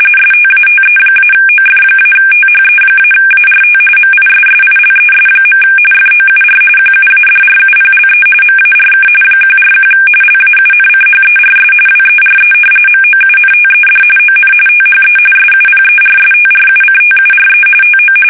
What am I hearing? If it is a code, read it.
RTTY, about 45.45 baud. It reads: STATE-OF-THE-ART MISSILES. FOUR QUADRUPLE MARK 141 LAUNCHERS WERE MOUNTED ON THE SUPERSTRUCTURE TO
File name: fsk850 / battleshipnj